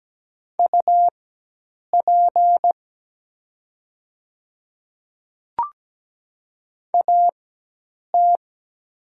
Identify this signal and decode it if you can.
Morse code, heard as a keyed tone, about 17 wpm, spaced as for 8 wpm, 700 Hz. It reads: UP AT
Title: 4x Character Spacing - Top 100 Words - 17wpm